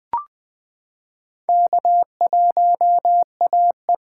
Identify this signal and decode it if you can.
Morse code, heard as a keyed tone, about 20 wpm, 700 Hz. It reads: K1AE
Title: K1USN SST QSO Practice - 20wpm